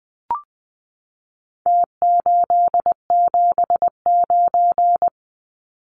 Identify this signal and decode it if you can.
Morse code, heard as a keyed tone, about 20 wpm, 700 Hz. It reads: T879
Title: Numbers - Contest Style - 4-Digit 20wpm